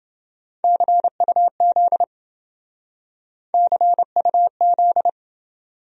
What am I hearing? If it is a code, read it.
Morse code, heard as a keyed tone, about 29 wpm, 700 Hz. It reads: CUZ CUZ